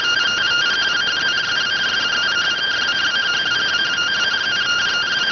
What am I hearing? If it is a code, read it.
Two-tone teleprinter signal, about 170 Hz shift.